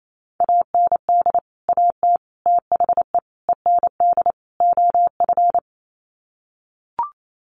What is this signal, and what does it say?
Morse code, heard as a keyed tone, about 28 wpm, 700 Hz. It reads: AND AT THE END OF
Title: Sets of 5 Words 28wpm